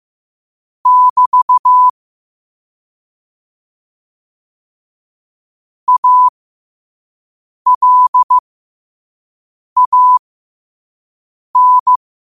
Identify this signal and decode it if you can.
Morse code, heard as a keyed tone, about 15 wpm, spaced as for 5 wpm, 1000 Hz. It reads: <BT> ALAN
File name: qotdfiles / 5wpm